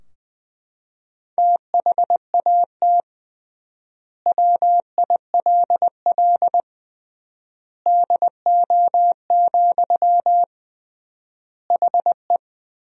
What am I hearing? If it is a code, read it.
Morse code, heard as a keyed tone, about 21 wpm, 700 Hz. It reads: THAT WILL DO, HE
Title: 3x Word-Spacing - Sentences from Top 100 Words - 20wpm